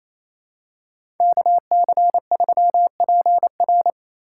Morse code, heard as a keyed tone, about 28 wpm, 700 Hz. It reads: KC3PR
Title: USA Call Signs - 5 Characters - 6x Character Spacing - 28wpm